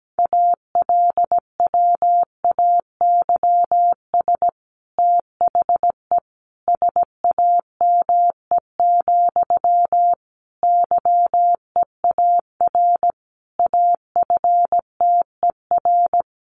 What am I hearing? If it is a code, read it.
Morse code, heard as a keyed tone, about 17 wpm, 700 Hz. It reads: ALWAYS THE SAME, YEAR AFTER
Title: Sentences from Top 300 Words 17wpm